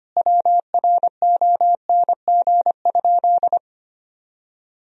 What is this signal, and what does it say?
Morse code, heard as a keyed tone, about 25 wpm, 700 Hz. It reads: WRONG?
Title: Sentences from Top 1000 Words - Review Part 1 - 25wpm